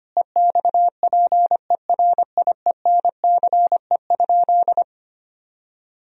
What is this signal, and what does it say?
Morse code, heard as a keyed tone, about 25 wpm, 700 Hz. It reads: EXPERIENCE?
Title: Sentences from Top 1000 Words - Review Part 3 - 25wpm